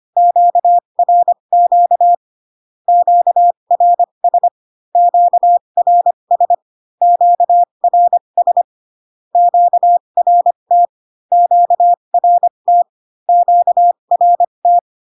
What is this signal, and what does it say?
Morse code, heard as a keyed tone, about 25 wpm, spaced as for 22 wpm, 700 Hz. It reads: QRQ QRS QRS QRS QRT QRT QRT
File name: Q-codes most common repeated 3 times 18 25wpm